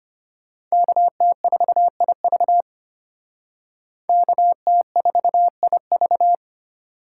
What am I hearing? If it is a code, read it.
Morse code, heard as a keyed tone, about 32 wpm, 700 Hz. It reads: KT4IV KT4IV